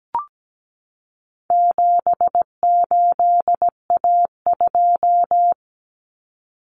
Morse code, heard as a keyed tone, about 17 wpm, 700 Hz. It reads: 78A2
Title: Common Cut Numbers - 4-Digit 17wpm